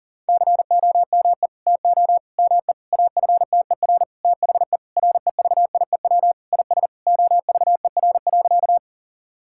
Morse code, heard as a keyed tone, about 40 wpm, 690 Hz. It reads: COME TO ME AFTER THE REVIEW IS OVER.
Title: Sentences from Top 1000 Words - Review Part 6 - 40wpm